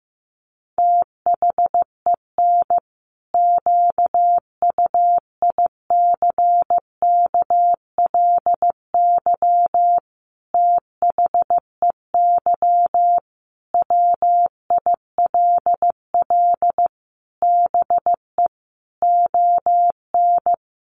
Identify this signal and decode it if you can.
Morse code, heard as a keyed tone, about 15 wpm, 700 Hz. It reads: THEN QUICKLY THEY WILL BE ON
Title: Sentences from Top 700 Words 15wpm